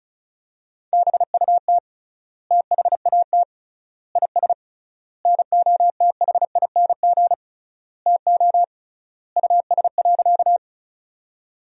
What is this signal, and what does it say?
Morse code, heard as a keyed tone, about 36 wpm, 700 Hz. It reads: BUT THAT IS NOTHING TO US.
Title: 3x Word-Spacing - Sentences from Top 500 Words - 35wpm